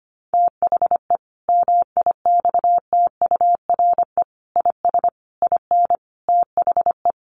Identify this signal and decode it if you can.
Morse code, heard as a keyed tone, about 25 wpm, 700 Hz. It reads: THE MIXTURE IS IN THE
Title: Sentences from Top 1500 Words 25wpm